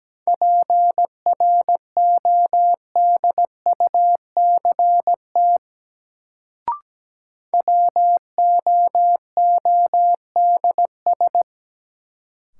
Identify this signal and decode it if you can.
Morse code, heard as a keyed tone, about 17 wpm, 700 Hz. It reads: PRODUCT WOODS
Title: Sentences from Top 1000 Words - Review Part 2 - New Word Drill - 17wpm